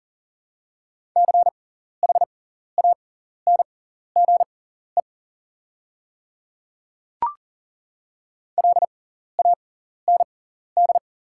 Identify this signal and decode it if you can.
Morse code, heard as a keyed tone, about 41 wpm, spaced as for 14 wpm, 690 Hz. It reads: CHANGE LAND
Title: 6x Character Spacing - Top 200 Words - 40wpm